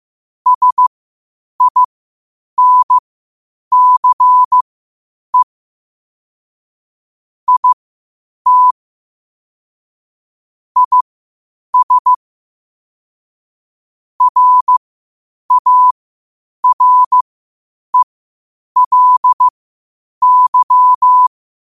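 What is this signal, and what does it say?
Morse code, heard as a keyed tone, about 15 wpm, spaced as for 8 wpm, 1000 Hz. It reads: SINCE IT IS RARELY
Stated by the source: Morse code at 8 wpm.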